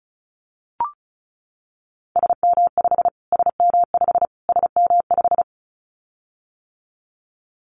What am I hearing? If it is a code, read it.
Morse code, heard as a keyed tone, about 34 wpm, 700 Hz. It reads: SM5 SM5 SM5
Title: Callsign Prefixes for Top 25 DXCC - Repeated 3x - 35wpm